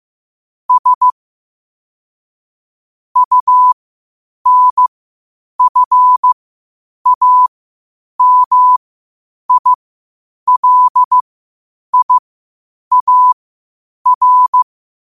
Morse code, heard as a keyed tone, about 15 wpm, spaced as for 8 wpm, 1000 Hz. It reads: S UNFAMILIAR
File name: qotdfiles / 8wpm